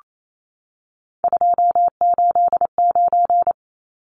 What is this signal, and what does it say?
Morse code, heard as a keyed tone, about 28 wpm, 700 Hz. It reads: 289
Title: Numbers - 3-Digit 28wpm